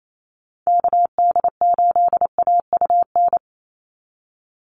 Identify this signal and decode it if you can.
Morse code, heard as a keyed tone, about 28 wpm, 700 Hz. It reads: KD8AUN
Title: USA Call Signs 28wpm